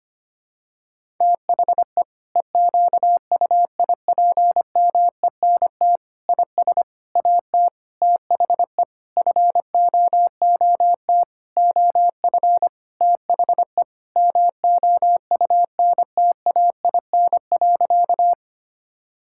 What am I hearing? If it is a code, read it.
Morse code, heard as a keyed tone, about 25 wpm, 700 Hz. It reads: THE EQUIPMENT IS AT THE FOOT OF THE MOUNTAIN.
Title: Sentences from Top 1000 Words - Review Part 5 - 25wpm